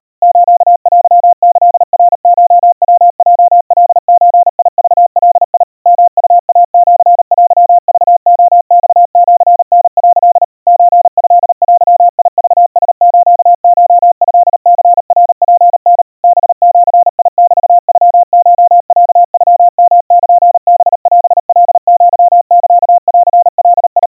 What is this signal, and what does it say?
Morse code (Japanese Wabun code), heard as a keyed tone, about 38 wpm, 700 Hz. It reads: ステキナコヤヲカゾクガ ヨウイシテクレマシタ」 ソトデクラスコトニナツタ ハジメノコロノヨルハオカアサンガ
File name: kawa / Morse sample / kamyu 140 cpm 700 Hz Crispy